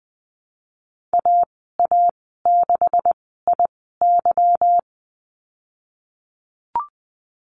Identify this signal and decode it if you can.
Morse code, heard as a keyed tone, about 20 wpm, spaced as for 15 wpm, 700 Hz. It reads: AA6IY
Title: 186 Lesson 30g - Letter b - Call Signs 20wpm